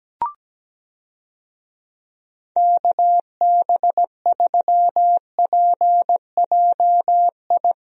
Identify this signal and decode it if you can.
Morse code, heard as a keyed tone, about 17 wpm, 700 Hz. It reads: KB3PJI
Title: USA Call Signs - 6 Characters - Repeated 2x - 17wpm